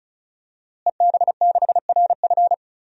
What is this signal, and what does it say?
Morse code, heard as a keyed tone, about 34 wpm, 700 Hz. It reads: EB6RF
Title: DXCC-100 Call Signs - 5 Characters - Repeated 3x - 35wpm